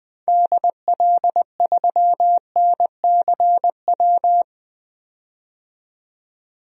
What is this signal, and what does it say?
Morse code, heard as a keyed tone, about 20 wpm, 700 Hz. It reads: DL3NCW
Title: DXCC-100 Call Signs - 6 Characters - 20wpm